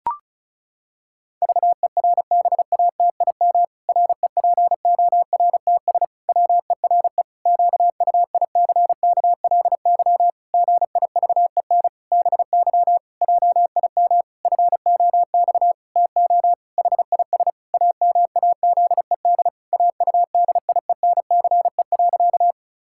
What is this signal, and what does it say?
Morse code, heard as a keyed tone, about 34 wpm, 700 Hz. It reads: VERBATIM REPORTS WERE QUICKLY GIVEN BY JIM FOX TO HIS AMAZED AUDIENCE.
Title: Pangram Challenge v2 - 35wpm